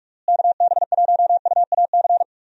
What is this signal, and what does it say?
Morse code, heard as a keyed tone, about 42 wpm, 700 Hz. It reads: KB1UAC
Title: USA Call Signs - 6 Characters - Repeated 3x - 45wpm